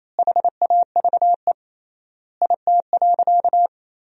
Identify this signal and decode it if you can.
Morse code, heard as a keyed tone, about 28 wpm, 700 Hz. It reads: HAVE IT.
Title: 3x Word-Spacing - Sentences from Top 100 Words - 28wpm